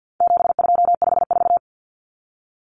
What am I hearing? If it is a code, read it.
Morse code, heard as a keyed tone, about 48 wpm, 700 Hz. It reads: 7254
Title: Common Cut Numbers - 4-Digit 50wpm